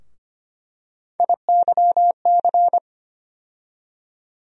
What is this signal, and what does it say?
Morse code, heard as a keyed tone, about 25 wpm, 700 Hz. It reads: IYC
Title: USA Call Sign Postfixes - 3 Letters 25wpm